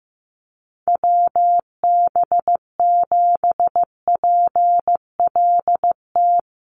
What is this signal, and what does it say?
Morse code, heard as a keyed tone, about 15 wpm, 700 Hz. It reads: WB7PLT
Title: USA Call Signs Repeated 3x 15wpm